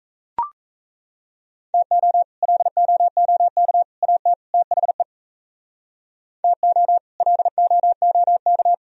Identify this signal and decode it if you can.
Morse code, heard as a keyed tone, about 36 wpm, 700 Hz. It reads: TO LOOK AT THE TO LOOK